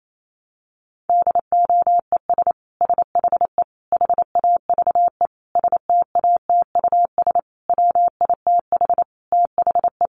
Morse code, heard as a keyed tone, about 28 wpm, 700 Hz. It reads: DOES SHE HAVE STATUS WITH THE
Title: Sentences from Top 1400 Words 28wpm